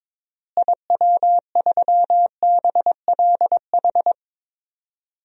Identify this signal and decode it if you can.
Morse code, heard as a keyed tone, about 22 wpm, 700 Hz. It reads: IW3BLH
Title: DXCC-100 Call Signs - 6 Characters - Repeated 3x - 22wpm